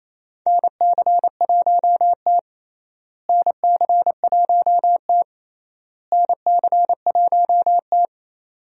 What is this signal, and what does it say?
Morse code, heard as a keyed tone, about 28 wpm, 700 Hz. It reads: NC1T NC1T NC1T
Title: USA Call Signs - 4 Characters - Repeated 3x - 28wpm